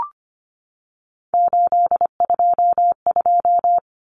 Morse code, heard as a keyed tone, about 25 wpm, 700 Hz. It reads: 822
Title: Common Cut Numbers - 3-Digit 25wpm